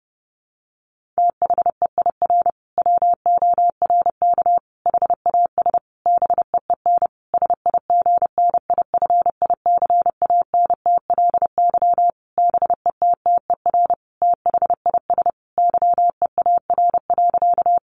Morse code, heard as a keyed tone, about 30 wpm, 700 Hz. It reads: THEIR WORK HAS BEEN SIGNIFICANTLY BETTER THIS YEAR.
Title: Sentences from Top 1500 Words 30wpm